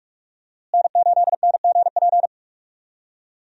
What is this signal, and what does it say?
Morse code, heard as a keyed tone, about 47 wpm, 700 Hz. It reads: N8NGP
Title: USA Call Signs - 5 Characters - Repeated 2x - 45wpm